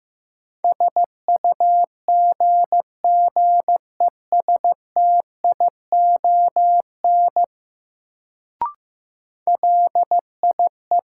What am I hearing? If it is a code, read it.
Morse code, heard as a keyed tone, about 15 wpm, 700 Hz. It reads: SUGGESTION LIE
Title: Sentences from Top 1600 Words - New Word Drill Alt - 15wpm